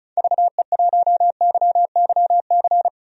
Morse code, heard as a keyed tone, about 34 wpm, 700 Hz. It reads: VE1YYC
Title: DXCC-100 Call Signs - 6 Characters - 35wpm